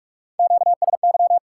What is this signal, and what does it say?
Morse code, heard as a keyed tone, about 59 wpm, spaced as for 52 wpm, 700 Hz. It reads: QSY
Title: Speed-Racing Top 90 QSO Elements 45wpm